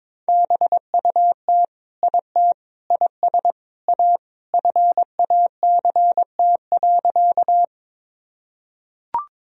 Morse code, heard as a keyed tone, about 22 wpm, 700 Hz. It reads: BUT IT IS A FACT.
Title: Sentences from Top 1000 Words - Review Part 3 - 22wpm